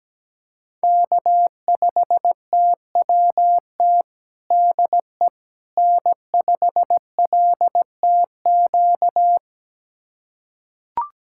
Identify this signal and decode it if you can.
Morse code, heard as a keyed tone, about 17 wpm, 700 Hz. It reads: K5TWT DE N5LTQ
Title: USA Callsign DE Callsign - 5 Characters - 17wpm